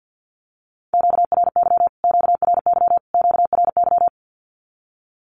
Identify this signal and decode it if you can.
Morse code, heard as a keyed tone, about 50 wpm, 700 Hz. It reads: QRY QRY QRY
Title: Q-Codes Repeated 3x 50wpm